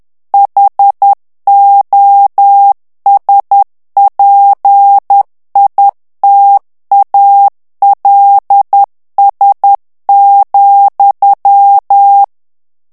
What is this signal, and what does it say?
Morse code, heard as a keyed tone, about 10 wpm, 800 Hz. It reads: HOSPITALS,
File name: morse mp3 / ANGLAIS texte clair 2 vitesse 10 mots minute